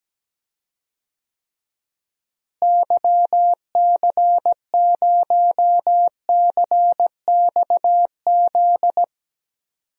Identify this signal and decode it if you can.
Morse code, heard as a keyed tone, about 17 wpm, 700 Hz. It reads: YC0CXZ
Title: DXCC-100 Call Signs - 6 Characters - Repeated 2x - 17wpm